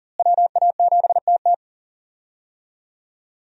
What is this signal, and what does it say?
Morse code, heard as a keyed tone, about 40 wpm, 690 Hz. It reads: WA7TT
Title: USA Call Signs - 5 Characters - Repeated 3x - 40wpm